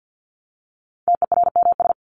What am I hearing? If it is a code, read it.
Morse code, heard as a keyed tone, about 50 wpm, 700 Hz. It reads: TERMS
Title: ES:S Suffix - Top 100 Words with Base Words 50wpm